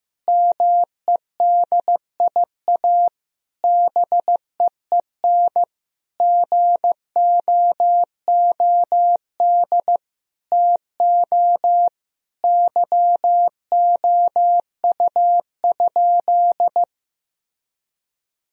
Morse code, heard as a keyed tone, about 15 wpm, 700 Hz. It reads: MEDIA BEEN GOOD TO YOU?
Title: Sentences from Top 1000 Words - Review Part 4 - 15wpm